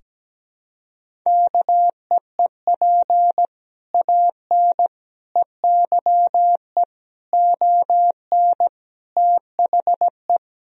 Morse code, heard as a keyed tone, about 17 wpm, 700 Hz. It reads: KEEP AN EYE ON THE
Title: Sentences from Top 1000 Words - Review Part 3 - 17wpm